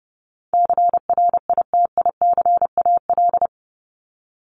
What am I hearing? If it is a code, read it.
Morse code, heard as a keyed tone, about 30 wpm, 700 Hz. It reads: CRITICAL
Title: Top 1001-1250 Words 30wpm